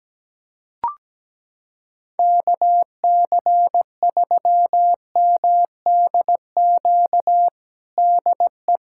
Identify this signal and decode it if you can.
Morse code, heard as a keyed tone, about 17 wpm, 700 Hz. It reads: KC3MDQ DE
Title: USA Callsign DE Callsign 17wpm